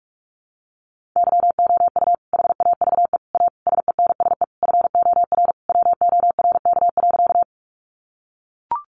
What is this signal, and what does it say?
Morse code, heard as a keyed tone, about 44 wpm, 700 Hz. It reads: YOU HAVE A SENSE FOR WORK.
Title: Sentences from Top 700 Words 45wpm